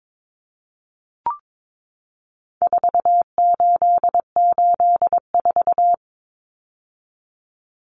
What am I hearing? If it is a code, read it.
Morse code, heard as a keyed tone, about 22 wpm, 700 Hz. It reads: 4884
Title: Common Cut Numbers - 4-Digit 22wpm